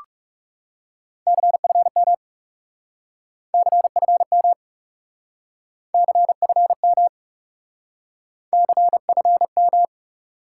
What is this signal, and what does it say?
Morse code, heard as a keyed tone, about 42 wpm, 700 Hz. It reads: CFM CFM CFM CFM